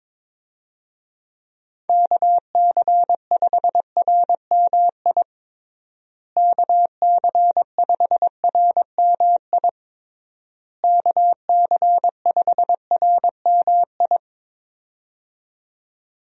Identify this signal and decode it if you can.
Morse code, heard as a keyed tone, about 22 wpm, 700 Hz. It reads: KC5RMI KC5RMI KC5RMI
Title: USA Call Signs - 6 Characters - Repeated 3x - 22wpm